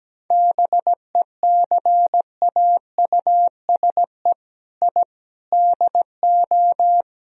Morse code, heard as a keyed tone, about 17 wpm, 700 Hz. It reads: BECAUSE I DO
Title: Speed-Racing - Sets of 4 Words 17wpm